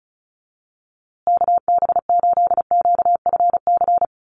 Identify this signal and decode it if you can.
Morse code, heard as a keyed tone, about 34 wpm, 700 Hz. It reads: KB8QFC
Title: USA Call Signs Repeated 2x 35wpm